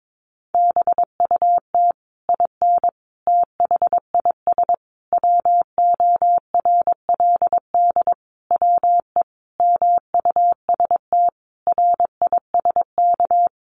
Morse code, heard as a keyed tone, about 22 wpm, 700 Hz. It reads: BUT IN THIS WORLD WE MUST RISK
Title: Sentences from Top 600 Words 22wpm